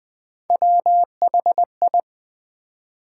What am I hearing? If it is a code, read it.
Morse code, heard as a keyed tone, about 20 wpm, 700 Hz. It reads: WHI
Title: Build A Word - 1 Letter At A Time - 5-Letter Words - 20wpm